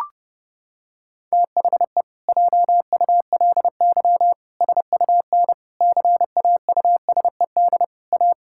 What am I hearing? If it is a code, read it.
Morse code, heard as a keyed tone, about 30 wpm, 700 Hz. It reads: THE JULY SUN CAUSED A
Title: Pangram Challenge v2 - 30wpm